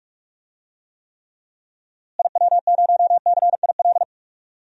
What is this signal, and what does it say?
Morse code, heard as a keyed tone, about 44 wpm, 700 Hz. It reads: IW0CIL
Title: DXCC-100 Call Signs - 6 Characters - Repeated 2x - 45wpm